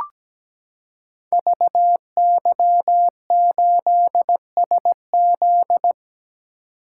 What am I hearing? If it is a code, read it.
Morse code, heard as a keyed tone, about 17 wpm, 700 Hz. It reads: VY8SZ
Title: Code Groups - 5 Letter:Numbers - 17wpm